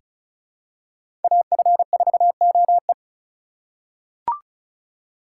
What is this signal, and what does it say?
Morse code, heard as a keyed tone, about 34 wpm, 700 Hz. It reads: AF4OE
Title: USA Call Signs - 5 Characters - 3x Character Spacing - 35wpm